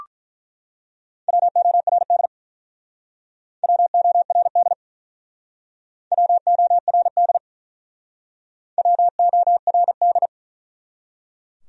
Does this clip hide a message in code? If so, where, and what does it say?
Yes: Morse code from 1.3 to 10.3 s, heard as a keyed tone, about 48 wpm, 700 Hz. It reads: WORD WORD WORD WORD